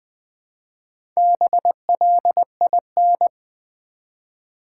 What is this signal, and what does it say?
Morse code, heard as a keyed tone, about 20 wpm, 700 Hz. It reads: BLIN
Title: Build A Word - 1 Letter At A Time - 5-Letter Words - 20wpm